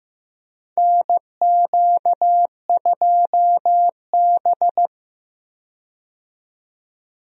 Morse code, heard as a keyed tone, about 16 wpm, 700 Hz. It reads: NQ2B
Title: USA Call Signs - 4 Characters - 15wpm